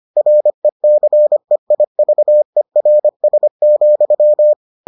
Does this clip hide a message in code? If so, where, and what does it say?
Yes: Morse code from 0.2 to 4.5 s, heard as a keyed tone, about 25 wpm, 600 Hz. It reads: RECEIVERS,